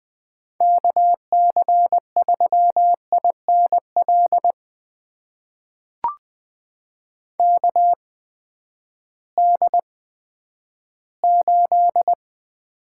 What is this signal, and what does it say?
Morse code, heard as a keyed tone, about 21 wpm, spaced as for 11 wpm, 700 Hz. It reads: KC3INL KD8
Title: USA Call Signs - 6 Characters - 8x Character Spacing - 20wpm